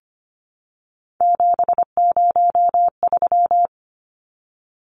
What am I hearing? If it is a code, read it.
Morse code, heard as a keyed tone, about 25 wpm, 700 Hz. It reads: 703
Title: Numbers - 3-Digit 25wpm